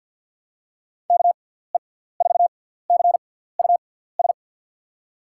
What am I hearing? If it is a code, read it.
Morse code, heard as a keyed tone, about 47 wpm, spaced as for 17 wpm, 700 Hz. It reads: KE4CUS
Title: USA Call Signs - 6 Characters - 6x Character Spacing - 50wpm